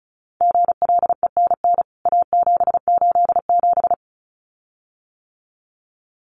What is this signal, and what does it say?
Morse code, heard as a keyed tone, about 34 wpm, 700 Hz. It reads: GLENN A787
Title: CWT Exchange 35wpm